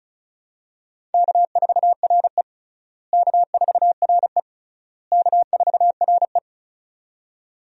Morse code, heard as a keyed tone, about 34 wpm, 700 Hz. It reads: K4RE K4RE K4RE
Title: USA Call Signs - 4 Characters - Repeated 3x - 35wpm